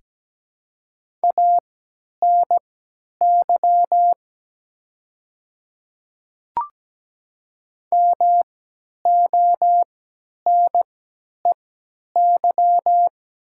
Morse code, heard as a keyed tone, about 17 wpm, spaced as for 10 wpm, 700 Hz. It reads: ANY MONEY
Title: 3x Character Spacing - Top 500 Words - 17wpm